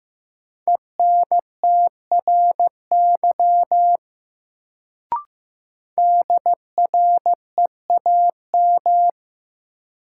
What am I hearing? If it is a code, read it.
Morse code, heard as a keyed tone, about 15 wpm, 700 Hz. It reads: ENTRY DREAM
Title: Sentences from Top 1500 Words - New Word Drill - 15wpm